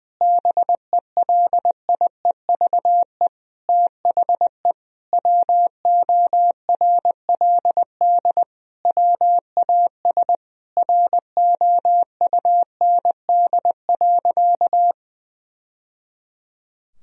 Morse code, heard as a keyed tone, about 20 wpm, 700 Hz. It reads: BELIEVE THE WORLD WAS ROUND.
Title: Sentences from Top 1000 Words - Review Part 1 - 20wpm